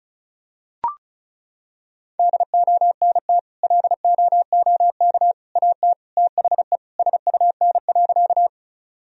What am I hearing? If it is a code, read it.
Morse code, heard as a keyed tone, about 36 wpm, 700 Hz. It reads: DONT LOOK AT THE SUN.
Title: Sentences from Top 1000 Words - Review Part 1 - 35wpm